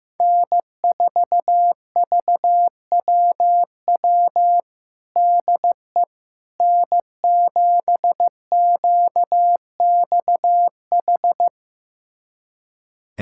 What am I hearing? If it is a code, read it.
Morse code, heard as a keyed tone, about 15 wpm, 700 Hz. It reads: N4VWW DE N7QXH
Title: USA Callsign DE Callsign - 5 Characters - 15wpm